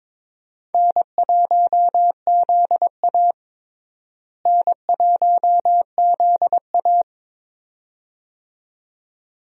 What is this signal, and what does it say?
Morse code, heard as a keyed tone, about 22 wpm, 700 Hz. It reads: N1ZA N1ZA
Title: USA Call Signs - 4 Characters - Repeated 2x - 22wpm